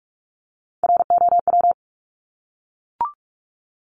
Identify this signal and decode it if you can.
Morse code, heard as a keyed tone, about 44 wpm, 700 Hz. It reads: ROW
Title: Top 1751-2000 Words 45wpm